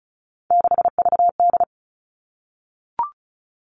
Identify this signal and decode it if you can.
Morse code, heard as a keyed tone, about 34 wpm, 700 Hz. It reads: 6VD
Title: All Cut Numbers - 3-Digit 35wpm